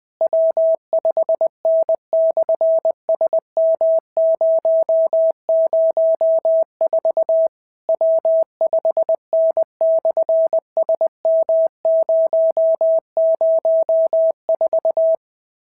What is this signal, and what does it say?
Morse code, heard as a keyed tone, about 20 wpm, 650 Hz. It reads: W5N/SM004 W5N/SM004
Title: SOTA QSO Practice 20wpm